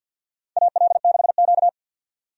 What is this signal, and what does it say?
Morse code, heard as a keyed tone, about 52 wpm, 700 Hz. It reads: AL6Q
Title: USA Call Signs - 4 Characters - 50wpm